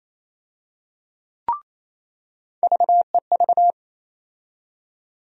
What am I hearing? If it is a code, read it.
Morse code, heard as a keyed tone, about 28 wpm, 700 Hz. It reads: VEV
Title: USA Call Sign Postfixes - 3 Letters 28wpm